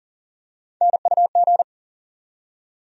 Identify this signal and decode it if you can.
Morse code, heard as a keyed tone, about 40 wpm, 700 Hz. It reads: NUG
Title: USA Call Sign Postfixes - 3 Letters 40wpm